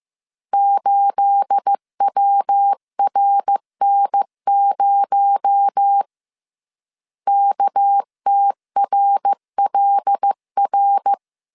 Morse code, heard as a keyed tone, about 15 wpm, 800 Hz. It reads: 8WRN0 KTRLR
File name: CW / DET / 1 9 1500C S B 14wpm 3302